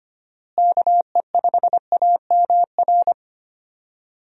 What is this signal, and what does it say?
Morse code, heard as a keyed tone, about 25 wpm, 700 Hz. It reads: KE5AMR
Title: USA Call Signs - 6 Characters - Repeated 3x - 25wpm